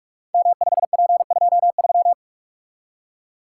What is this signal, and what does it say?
Morse code, heard as a keyed tone, about 44 wpm, 690 Hz. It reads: M5P23
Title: Code Groups - 5 Letter:Numbers - 45wpm